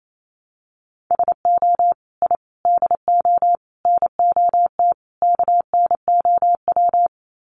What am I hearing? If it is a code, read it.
Morse code, heard as a keyed tone, about 28 wpm, 700 Hz. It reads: SO I DO NOT KNOW
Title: Sets of 5 Words Encore 28wpm